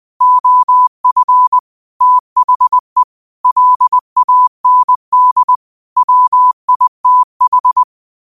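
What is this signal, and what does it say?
Morse code, heard as a keyed tone, about 20 wpm, 1000 Hz. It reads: OF THE LAND WITH
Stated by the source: Morse code practice at 20 wpm: famous quotations sent in telegraphy, from URVAG.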